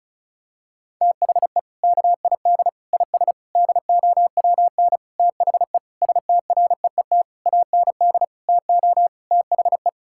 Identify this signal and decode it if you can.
Morse code, heard as a keyed tone, about 36 wpm, 700 Hz. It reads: THE KID IS DOWN THE STREET AND TO THE
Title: Sentences from Top 1600 Words 35wpm